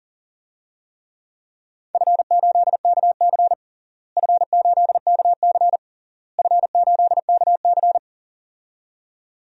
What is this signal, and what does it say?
Morse code, heard as a keyed tone, about 37 wpm, 700 Hz. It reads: F8KC F8KC F8KC
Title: DXCC-100 Call Signs - 4 Characters - Repeated 3x - 40wpm